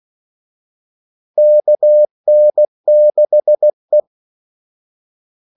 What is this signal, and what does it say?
Morse code, heard as a keyed tone, about 16 wpm, 600 Hz. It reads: KN6E
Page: Kaldesignaler 16 wpm